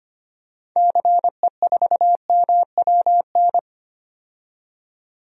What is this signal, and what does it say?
Morse code, heard as a keyed tone, about 25 wpm, 700 Hz. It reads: CE4MWN
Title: DXCC-100 Call Signs - 6 Characters - 25wpm